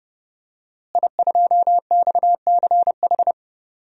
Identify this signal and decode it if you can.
Morse code, heard as a keyed tone, about 30 wpm, 700 Hz. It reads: I2XCH
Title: DXCC-100 Call Signs - 5 Characters - Repeated 3x - 30wpm